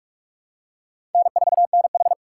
Morse code, heard as a keyed tone, about 46 wpm, 700 Hz. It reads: N4NH
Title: USA Call Signs - 4 Characters - 4x Character Spacing - 45wpm